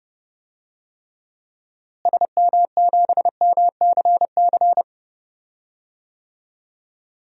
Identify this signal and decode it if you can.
Morse code, heard as a keyed tone, about 32 wpm, 700 Hz. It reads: SM7MCC
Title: DXCC-100 Call Signs - 6 Characters - Repeated 2x - 30wpm